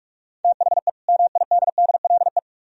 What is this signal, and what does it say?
Morse code, heard as a keyed tone, about 56 wpm, 700 Hz. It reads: THE MIDDLE
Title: Speed-Racing - Sets of 2 Words 45wpm